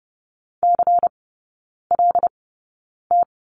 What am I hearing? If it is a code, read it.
Morse code, heard as a keyed tone, about 30 wpm, spaced as for 9 wpm, 700 Hz. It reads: CLT
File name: Lesson 16e - Letter c - Three Characters 30wpm